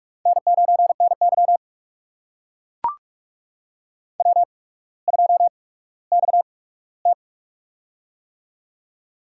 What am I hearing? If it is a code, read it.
Morse code, heard as a keyed tone, about 44 wpm, spaced as for 19 wpm, 690 Hz. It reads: N9NY W2XT
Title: USA Call Signs - 4 Characters - 8x Character Spacing - 45wpm